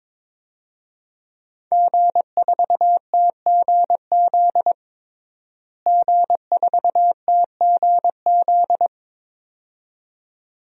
Morse code, heard as a keyed tone, about 21 wpm, 700 Hz. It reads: G4TGZ G4TGZ
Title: DXCC-100 Call Signs - 5 Characters - Repeated 2x - 22wpm